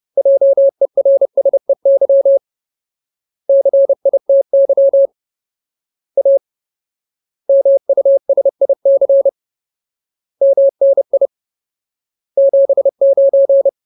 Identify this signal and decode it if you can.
Morse code, heard as a keyed tone, about 30 wpm, 550 Hz. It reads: JERSEY CITY A MUSIC MNI 79
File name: Ultimate Word List / 30WPM Ultimate